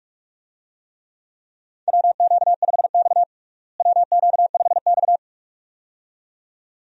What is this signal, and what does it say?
Morse code, heard as a keyed tone, about 46 wpm, 700 Hz. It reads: WQ5X WQ5X
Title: USA Call Signs - 4 Characters - Repeated 2x - 45wpm